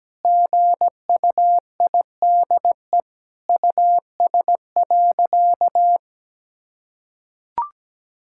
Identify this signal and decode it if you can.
Morse code, heard as a keyed tone, about 17 wpm, 700 Hz. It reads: GUIDE US.
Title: Sentences from Top 1000 Words - Review Part 5 - 17wpm